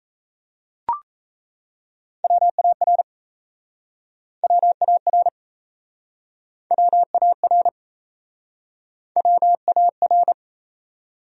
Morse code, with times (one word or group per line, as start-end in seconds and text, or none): 2.2-3.0 WAR
4.4-5.3 WAR
6.7-7.7 WAR
9.2-10.3 WAR